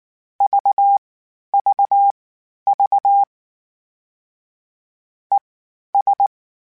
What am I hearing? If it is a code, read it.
Morse code, heard as a keyed tone, about 19 wpm, spaced as for 11 wpm, 800 Hz. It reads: VVV ES
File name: General / General-mp3s / Suffixes sp @18